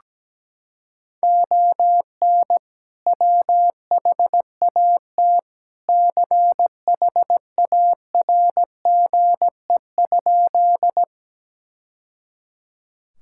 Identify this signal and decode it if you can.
Morse code, heard as a keyed tone, about 17 wpm, 700 Hz. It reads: ON WHAT CHARGE?
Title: Sentences from Top 1000 Words - Review Part 6 - 17wpm